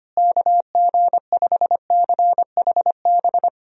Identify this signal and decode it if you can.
Morse code, heard as a keyed tone, about 25 wpm, 700 Hz. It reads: KG5CHB
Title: USA Call Signs - 6 Characters - Repeated 2x - 25wpm